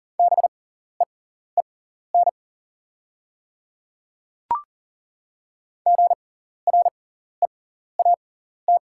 Morse code, heard as a keyed tone, about 40 wpm, spaced as for 14 wpm, 690 Hz. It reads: BEEN GREAT
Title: 6x Character Spacing - Top 200 Words - 40wpm